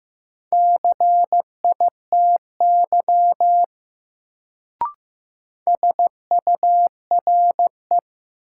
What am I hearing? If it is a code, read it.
Morse code, heard as a keyed tone, about 15 wpm, 700 Hz. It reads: CITY SURE
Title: Sentences from Top 300 Words - New Word Drill - 15wpm